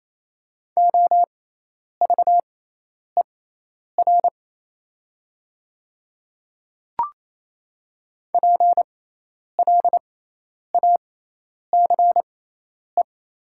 Morse code, heard as a keyed tone, about 27 wpm, spaced as for 10 wpm, 700 Hz. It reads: OVER PLACE
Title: 6x Character Spacing - Top 200 Words - 28wpm